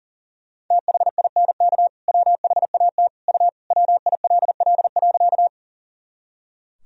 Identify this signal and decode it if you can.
Morse code, heard as a keyed tone, about 37 wpm, 700 Hz. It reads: THINK WHAT U WILL.
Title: Sentences from Top 200 Words and Abbreviations 40wpm